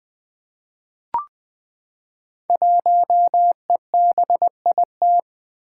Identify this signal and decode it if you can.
Morse code, heard as a keyed tone, about 20 wpm, 700 Hz. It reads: 1EBIT
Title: Five Challenging Letter-Number Combinations 20wpm